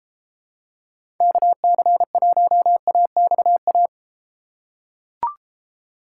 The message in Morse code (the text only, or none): KC1AXA